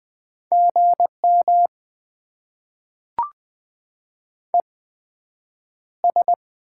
Morse code, heard as a keyed tone, about 20 wpm, spaced as for 9 wpm, 700 Hz. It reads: GM ES
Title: Top 20 QSO Elements - 8x Character Spacing - 20wpm